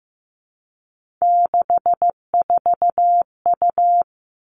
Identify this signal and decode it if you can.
Morse code, heard as a keyed tone, about 15 wpm, 700 Hz. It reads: 64U
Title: Three Challenging Letter-Number Combinations 15wpm